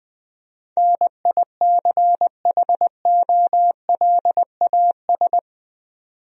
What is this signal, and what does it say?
Morse code, heard as a keyed tone, about 20 wpm, 700 Hz. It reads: NICHOLAS
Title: USA Common Names - 20wpm